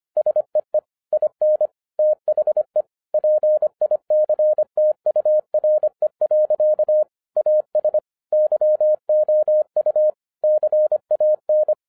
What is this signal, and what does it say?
Morse code, heard as a keyed tone, about 25 wpm, 610 Hz. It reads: SEE IN THE PICTURE. AS YOU CAN